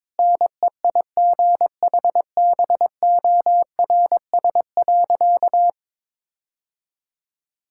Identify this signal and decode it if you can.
Morse code, heard as a keyed tone, about 21 wpm, 700 Hz. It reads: NEIGHBORS.
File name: practice / 22wpm / Sentences with 4-Word Phrases - 22wpm